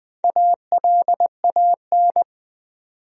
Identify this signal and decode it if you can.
Morse code, heard as a keyed tone, about 20 wpm, 700 Hz. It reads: ALAN
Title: USA Common Names - Max 4 Letters - 20wpm